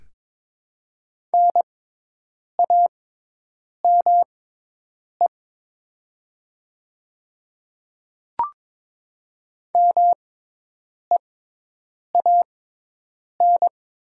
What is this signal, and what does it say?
Morse code, heard as a keyed tone, about 22 wpm, spaced as for 8 wpm, 700 Hz. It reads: NAME MEAN
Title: 6x Character Spacing - Top 200 Words - 22wpm